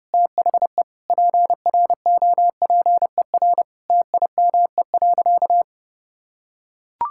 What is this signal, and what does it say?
Morse code, heard as a keyed tone, about 30 wpm, 700 Hz. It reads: THE PROPER TIME.
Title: Sentences from Top 1000 Words - Review Part 5 - 30wpm